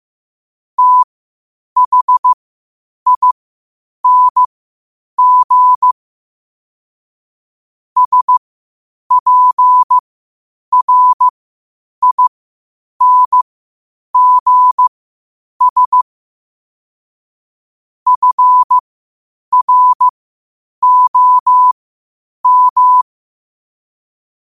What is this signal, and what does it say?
Morse code, heard as a keyed tone, about 15 wpm, spaced as for 8 wpm, 1000 Hz. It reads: THING SPRINGS FROM
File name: qotdfiles / 8wpm